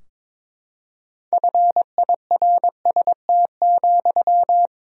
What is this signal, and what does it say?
Morse code, heard as a keyed tone, about 22 wpm, 700 Hz. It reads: FIRST,
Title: Sentences from Top 1000 Words - Review Part 3 - 22wpm